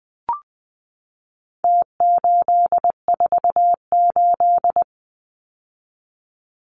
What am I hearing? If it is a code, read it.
Morse code, heard as a keyed tone, about 20 wpm, 700 Hz. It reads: T848
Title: Numbers - Contest Style - 4-Digit 20wpm